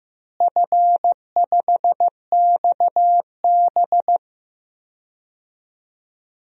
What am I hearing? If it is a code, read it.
Morse code, heard as a keyed tone, about 16 wpm, 700 Hz. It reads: F5XB
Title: DXCC-100 Call Signs - 4 Characters - Repeated 3x - 15wpm